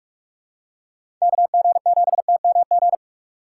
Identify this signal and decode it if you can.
Morse code, heard as a keyed tone, about 43 wpm, 700 Hz. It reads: KG7TMG
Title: USA Call Signs - 6 Characters - 4x Character Spacing - 45wpm